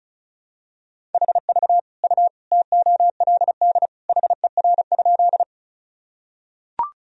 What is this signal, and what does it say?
Morse code, heard as a keyed tone, about 34 wpm, 700 Hz. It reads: HV U TOLD HER?
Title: Sentences from Top 400 Words and Abbreviations 35wpm